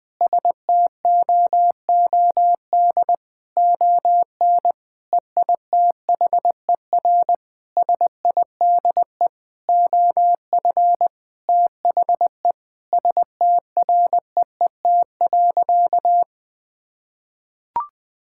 Morse code, heard as a keyed tone, about 20 wpm, 700 Hz. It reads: STOOD ON EITHER SIDE OF THE STREET.
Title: Sentences with 5-Word Phrases - 20wpm